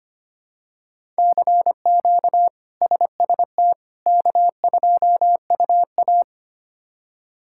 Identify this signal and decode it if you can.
Morse code, heard as a keyed tone, about 25 wpm, 700 Hz. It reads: CQ SST K2UA
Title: K1USN SST QSO Practice - 25wpm